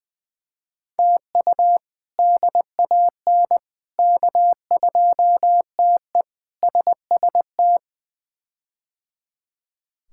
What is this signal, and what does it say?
Morse code, heard as a keyed tone, about 20 wpm, 700 Hz. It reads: TU DAN K2TE SST
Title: K1USN SST QSO Practice - 20wpm